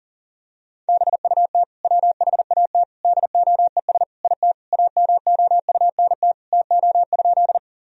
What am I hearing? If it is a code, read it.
Morse code, heard as a keyed tone, about 37 wpm, 700 Hz. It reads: BUT WHAT DOES IT AMOUNT TO?
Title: Sentences from Top 1000 Words - Review Part 4 - 40wpm